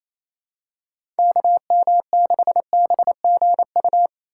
Morse code, heard as a keyed tone, about 28 wpm, 700 Hz. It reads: KM6BGU
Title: USA Call Signs - 6 Characters - 6x Character Spacing - 28wpm